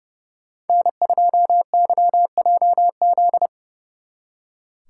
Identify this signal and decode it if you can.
Morse code, heard as a keyed tone, about 39 wpm, 700 Hz. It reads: N2YJZ